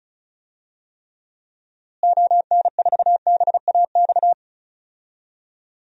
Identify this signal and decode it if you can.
Morse code, heard as a keyed tone, about 36 wpm, 700 Hz. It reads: ON4BAX
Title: DXCC-100 Call Signs - 6 Characters - Repeated 2x - 35wpm